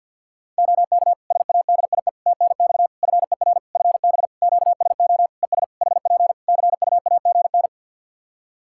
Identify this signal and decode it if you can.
Morse code, heard as a keyed tone, about 51 wpm, 700 Hz. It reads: OK SADIE TNX FER FB QSO ES HP CUAGN
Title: Practice QSO - Exchange 4 - Ending - 50wpm